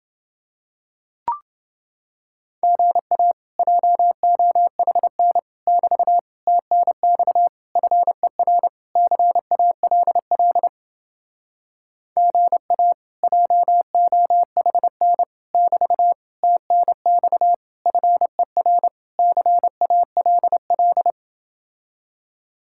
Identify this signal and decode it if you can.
Morse code, heard as a keyed tone, about 29 wpm, 700 Hz. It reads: GA JOHN <BT> TNX FER CALL GA JOHN <BT> TNX FER CALL